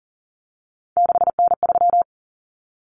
Morse code, heard as a keyed tone, about 40 wpm, 700 Hz. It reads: 6N3
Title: Common Cut Numbers - 3-Digit 40wpm